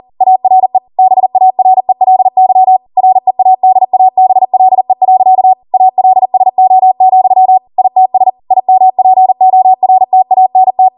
Morse code, heard as a keyed tone, about 40 wpm, 750 Hz. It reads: ARE BARELY READABLE. ALSO, ITS IMPORTANT